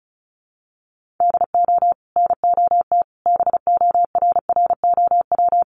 Morse code, heard as a keyed tone, about 34 wpm, 700 Hz. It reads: DO NOT BORROW
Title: Sentences from Top 1500 Words 35wpm